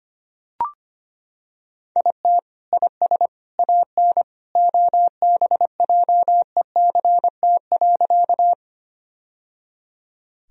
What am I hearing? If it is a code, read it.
Morse code, heard as a keyed tone, about 25 wpm, 700 Hz. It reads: IT IS AN OBJECT.
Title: Sentences from Top 1000 Words - Review Part 3 - 25wpm